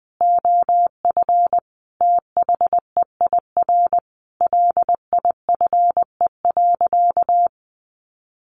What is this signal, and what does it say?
Morse code, heard as a keyed tone, about 20 wpm, 700 Hz. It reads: OF THEIR LIFE.
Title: Sentences from Top 1200 Words 20wpm